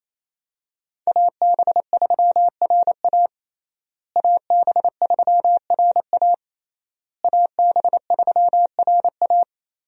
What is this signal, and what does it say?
Morse code, heard as a keyed tone, about 28 wpm, 700 Hz. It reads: AB3RA AB3RA AB3RA
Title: USA Call Signs - 5 Characters - Repeated 3x - 28wpm